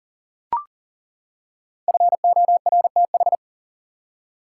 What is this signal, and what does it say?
Morse code, heard as a keyed tone, about 40 wpm, 700 Hz. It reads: FORTH
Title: Five Letter Words - 40wpm